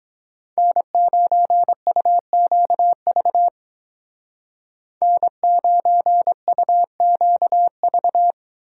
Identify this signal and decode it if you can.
Morse code, heard as a keyed tone, about 22 wpm, 700 Hz. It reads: N9UQV N9UQV